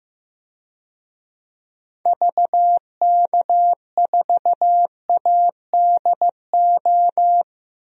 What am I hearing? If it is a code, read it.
Morse code, heard as a keyed tone, about 15 wpm, 700 Hz. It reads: VK4ADO
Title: DXCC-100 Call Signs - 6 Characters - 15wpm